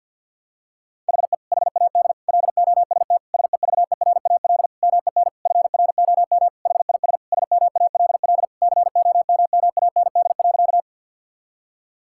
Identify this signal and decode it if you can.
Morse code, heard as a keyed tone, about 50 wpm, 700 Hz. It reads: HE HAD LOST SEVERAL MEN FROM HIS SMALL COMMAND.
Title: Sentences from Top 1600 Words 50wpm